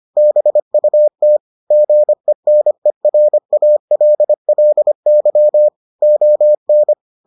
Morse code, heard as a keyed tone, about 25 wpm, 600 Hz. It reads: BUT GENERALLY ON